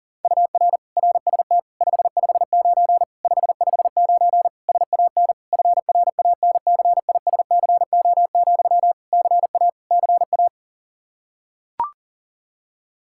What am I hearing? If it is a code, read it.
Morse code, heard as a keyed tone, about 40 wpm, 700 Hz. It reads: UR RST 559 559 SAN FRANCISCO, CA CA
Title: Practice QSO - Exchange 1 - Beginning - 40wpm